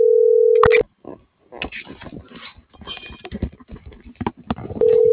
Ringback tone (North America), 440 + 480 Hz.